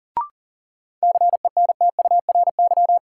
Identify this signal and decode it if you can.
Morse code, heard as a keyed tone, about 40 wpm, 700 Hz. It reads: CENTURY
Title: Sets of 5 Words - New Word Drill - 40wpm